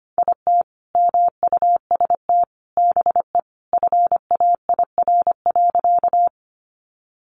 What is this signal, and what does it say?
Morse code, heard as a keyed tone, about 25 wpm, 700 Hz. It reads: IT MUST BE FAIR.
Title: Sentences from Top 1300 Words 25wpm